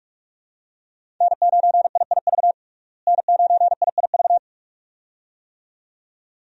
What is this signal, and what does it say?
Morse code, heard as a keyed tone, about 43 wpm, 700 Hz. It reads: N9IIV N9IIV
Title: USA Call Signs - 5 Characters - Repeated 2x - 45wpm